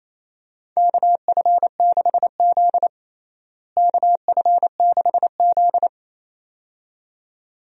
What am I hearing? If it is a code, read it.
Morse code, heard as a keyed tone, about 28 wpm, 700 Hz. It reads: KF6Z KF6Z
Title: USA Call Signs - 4 Characters - Repeated 2x - 28wpm